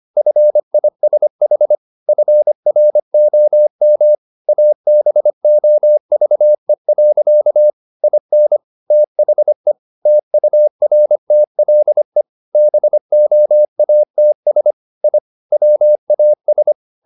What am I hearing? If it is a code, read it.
Morse code, heard as a keyed tone, about 25 wpm, 600 Hz. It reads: FISH FROM ABOVE. IN THE TURTLE BOATS I WAS